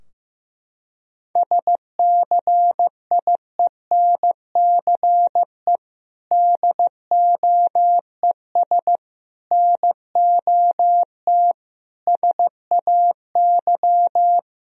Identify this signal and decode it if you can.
Morse code, heard as a keyed tone, about 15 wpm, 700 Hz. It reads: SCIENCE DOES NOT SAY
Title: Sentences from Top 1000 Words - Review Part 2 - 15wpm